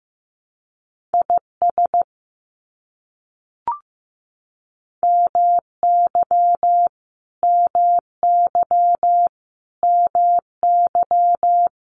Morse code, heard as a keyed tone, about 15 wpm, 700 Hz. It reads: IS MY MY MY
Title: Top 500 Words Repeated 15wpm